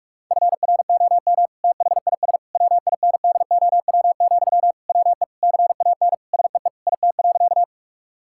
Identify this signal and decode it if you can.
Morse code, heard as a keyed tone, about 46 wpm, 700 Hz. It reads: FROM THIS WINDOW, WE CAN SEE IT.
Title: Sentences from Top 1600 Words 45wpm